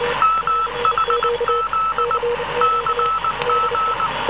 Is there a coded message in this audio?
Yes: Morse code from 0.2 to 4.1 s, heard as a keyed tone, about 19 wpm, 1310 Hz, with noise. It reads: MHZ CG